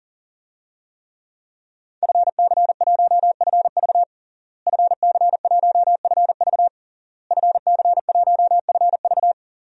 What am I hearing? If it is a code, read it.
Morse code, heard as a keyed tone, about 40 wpm, 700 Hz. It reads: FC1FV FC1FV FC1FV
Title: DXCC-100 Call Signs - 5 Characters - Repeated 3x - 40wpm